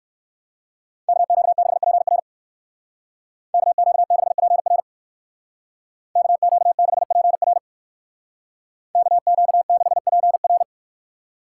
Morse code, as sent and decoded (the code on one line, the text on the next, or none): -.- --.- -.... .--. .-. / -.- --.- -.... .--. .-. / -.- --.- -.... .--. .-. / -.- --.- -.... .--. .-.
KQ6PR KQ6PR KQ6PR KQ6PR